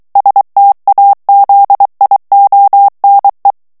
Morse code, heard as a keyed tone, about 23 wpm, 800 Hz. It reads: STAZIONE
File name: morse mp3 / ITALIEN texte clair 1 vitesse 22 mots minute